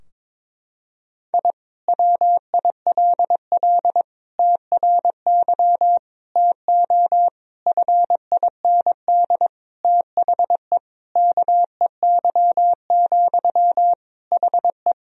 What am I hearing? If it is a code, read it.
Morse code, heard as a keyed tone, about 22 wpm, 700 Hz. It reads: I WILL TRY TO FIND THE KEY, HE
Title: Sentences from Top 1000 Words - Review Part 4 - 22wpm